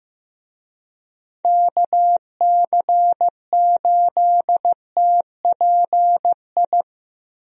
Morse code, heard as a keyed tone, about 15 wpm, 700 Hz. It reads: KC8TPI
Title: USA Call Signs - 6 Characters - Repeated 2x - 15wpm